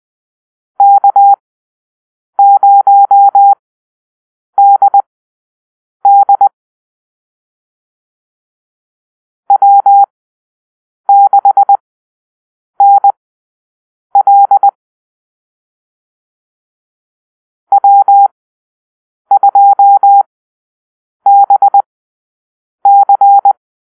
Morse code, heard as a keyed tone, about 20 wpm, spaced as for 7 wpm, 800 Hz. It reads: K0DD W6NL W2BC
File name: arrl / Call-Sign Drill-9